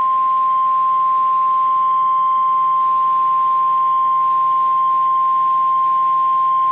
Emergency Alert System (SAME) tones (the 1050 Hz attention tone). It is heard from 1.1 to 6.0 s.